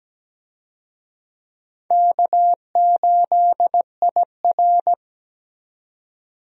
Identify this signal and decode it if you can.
Morse code, heard as a keyed tone, about 17 wpm, 700 Hz. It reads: K8IR
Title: USA Call Signs - 4 Characters - 17wpm